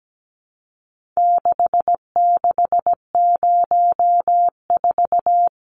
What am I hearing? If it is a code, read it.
Morse code, heard as a keyed tone, about 17 wpm, 700 Hz. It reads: 6604
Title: Numbers - 4-Digit 17wpm